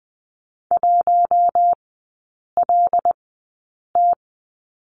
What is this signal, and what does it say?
Morse code, heard as a keyed tone, about 20 wpm, spaced as for 8 wpm, 700 Hz. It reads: 1LT
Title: 178 Lesson 29e - Slash - Three Characters 20wpm